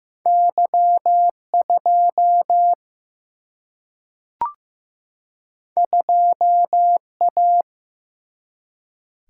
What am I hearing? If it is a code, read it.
Morse code, heard as a keyed tone, about 16 wpm, 700 Hz. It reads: Y 2 2 A
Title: Single-Letter Single-Number 15wpm